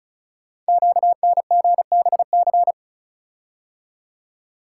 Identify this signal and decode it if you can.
Morse code, heard as a keyed tone, about 36 wpm, 700 Hz. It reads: QNGBC
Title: Code Groups - 5 Letters - 35wpm